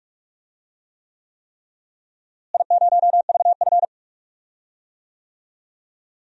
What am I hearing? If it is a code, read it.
Morse code, heard as a keyed tone, about 46 wpm, 700 Hz. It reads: I0VF
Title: DXCC-100 Call Signs - 4 Characters - Repeated 3x - 45wpm